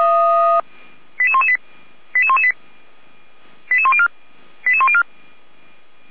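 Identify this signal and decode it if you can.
Five-tone selective call (ZVEI), heard as quick runs of five tones, tones of about 70 ms. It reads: 88118 88118 88104 88104
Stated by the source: ZVEI five-tone call 88192 received by SDR, recorded by Feuerwehr Orlamünde.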